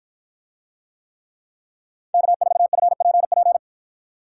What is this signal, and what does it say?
Morse code, heard as a keyed tone, about 53 wpm, 700 Hz. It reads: K4FPP